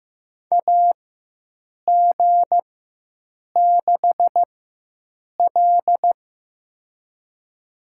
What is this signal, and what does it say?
Morse code, heard as a keyed tone, about 15 wpm, spaced as for 7 wpm, 700 Hz. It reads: AG6L
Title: USA Call Signs - 4 Characters - 4x Character Spacing - 15wpm